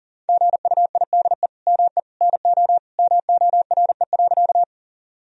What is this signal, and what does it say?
Morse code, heard as a keyed tone, about 37 wpm, 700 Hz. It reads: GUIDE ME NO MORE.
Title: Sentences from Top 1000 Words - Review Part 3 - 40wpm